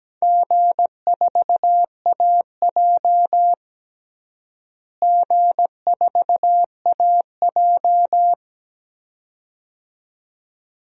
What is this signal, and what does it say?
Morse code, heard as a keyed tone, about 17 wpm, 700 Hz. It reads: G4AJ G4AJ
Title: DXCC-100 Call Signs - 4 Characters - Repeated 2x - 17wpm